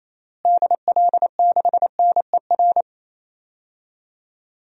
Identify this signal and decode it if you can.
Morse code, heard as a keyed tone, about 28 wpm, 700 Hz. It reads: DL6NER
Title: DXCC-100 Call Signs - 6 Characters - Repeated 3x - 28wpm